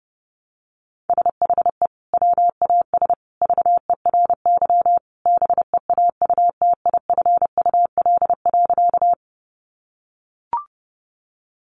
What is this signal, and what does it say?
Morse code, heard as a keyed tone, about 30 wpm, 700 Hz. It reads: SHE WAS VERY BEAUTIFUL.
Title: Sentences from Top 900 Words 30wpm